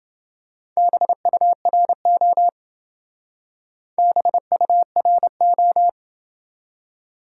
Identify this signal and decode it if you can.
Morse code, heard as a keyed tone, about 27 wpm, 700 Hz. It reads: BURO BURO